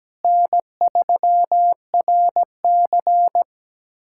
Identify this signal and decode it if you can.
Morse code, heard as a keyed tone, about 17 wpm, 700 Hz. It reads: N3RC
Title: USA Callsign DE Callsign - 4 Characters - 17wpm